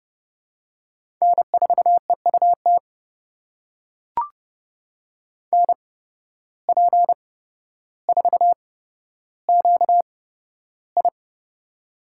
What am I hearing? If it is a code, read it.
Morse code, heard as a keyed tone, about 32 wpm, spaced as for 13 wpm, 700 Hz. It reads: N4EUT NP4QI
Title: USA Call Signs - 5 Characters - 8x Character Spacing - 30wpm